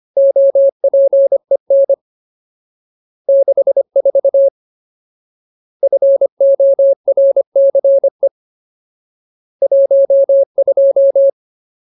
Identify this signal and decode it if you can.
Morse code, heard as a keyed tone, about 25 wpm, 550 Hz. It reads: OPEN 64 FORCE 12
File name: Ultimate Word List / 25WPM Ultimate